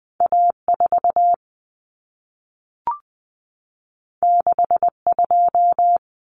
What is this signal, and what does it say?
Morse code, heard as a keyed tone, about 20 wpm, 700 Hz. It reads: A4 62
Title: Common Cut Numbers - 2-Digit 20wpm